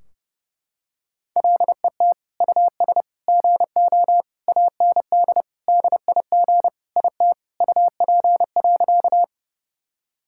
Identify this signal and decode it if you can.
Morse code, heard as a keyed tone, about 32 wpm, 700 Hz. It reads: LET US GO AND DIG IT UP.
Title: Sentences from Top 1600 Words 30wpm